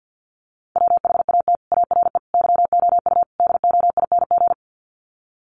Morse code, heard as a keyed tone, about 50 wpm, 700 Hz. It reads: WHAT ARE YOU DOING
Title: Sets of 4 Words 50wpm